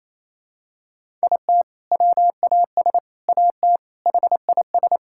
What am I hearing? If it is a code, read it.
Morse code, heard as a keyed tone, about 28 wpm, 700 Hz. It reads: IT WAS AT HIS
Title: Sentences from Top 1600 Words 28wpm